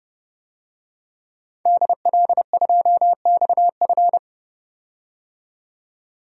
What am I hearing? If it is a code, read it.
Morse code, heard as a keyed tone, about 32 wpm, 700 Hz. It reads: DL2XF
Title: DXCC-100 Call Signs - 5 Characters - 30wpm